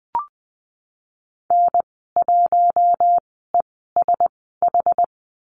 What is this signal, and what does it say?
Morse code, heard as a keyed tone, about 20 wpm, spaced as for 15 wpm, 700 Hz. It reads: N1ESH
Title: 082 Lesson 14g - Number 2 - Call Signs 20wpm